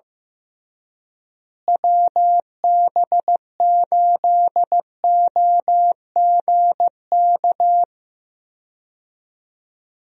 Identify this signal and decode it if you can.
Morse code, heard as a keyed tone, about 15 wpm, 700 Hz. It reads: WB8OGK
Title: USA Call Signs - 6 Characters - Repeated 3x - 15wpm